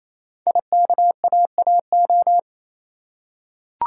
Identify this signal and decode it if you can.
Morse code, heard as a keyed tone, about 27 wpm, 700 Hz. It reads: IKAAO
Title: Code Groups - 5 Letters - 28wpm